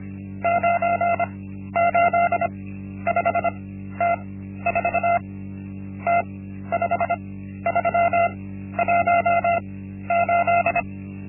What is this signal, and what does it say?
Morse code, heard as a keyed tone, about 25 wpm, spaced as for 12 wpm, 690 Hz. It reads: 985T4 T5318
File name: Linksmakalnis-1